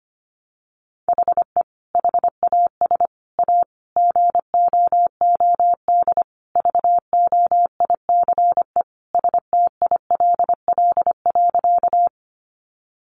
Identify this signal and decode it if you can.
Morse code, heard as a keyed tone, about 25 wpm, 700 Hz. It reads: HE HAS A GOOD VOICE STILL.
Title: Sentences from Top 500 Words 25wpm